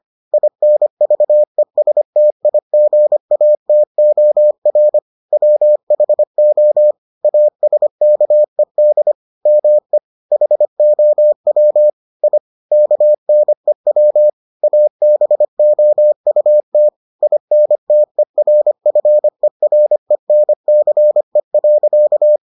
Morse code, heard as a keyed tone, about 25 wpm, 600 Hz. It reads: INVESTIGATOR WHO ASKED ME HOW I KNEW ABOUT INTERFERENCE.